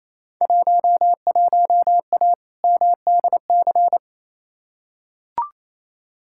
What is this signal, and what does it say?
Morse code, heard as a keyed tone, about 28 wpm, 700 Hz. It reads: 11A MDC
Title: Field Day Exchanges Encore - 28wpm